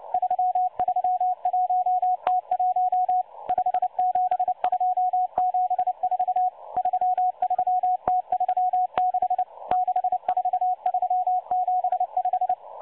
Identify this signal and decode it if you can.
Morse code, heard as a keyed tone, about 29 wpm, 730 Hz. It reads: 331T1 57274 33T36 64375
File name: M12-Morse-Numbers-Station-8116-kHz-CW